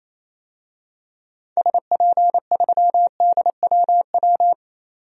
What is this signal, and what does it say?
Morse code, heard as a keyed tone, about 27 wpm, 700 Hz. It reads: SP3DWW
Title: DXCC-100 Call Signs - 6 Characters - 28wpm